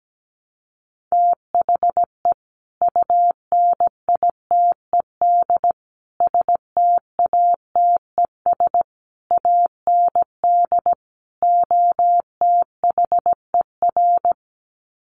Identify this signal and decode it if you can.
Morse code, heard as a keyed tone, about 17 wpm, 700 Hz. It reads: THE UNITED STATES AND OTHER
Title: Sets of 5 Words Encore 17wpm